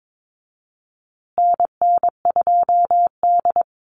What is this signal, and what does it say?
Morse code, heard as a keyed tone, about 22 wpm, 700 Hz. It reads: NN2D
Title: USA Call Signs Repeated 2x 22wpm